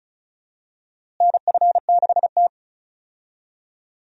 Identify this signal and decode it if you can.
Morse code, heard as a keyed tone, about 34 wpm, 700 Hz. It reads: NF6T
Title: USA Call Signs - 4 Characters - Repeated 2x - 35wpm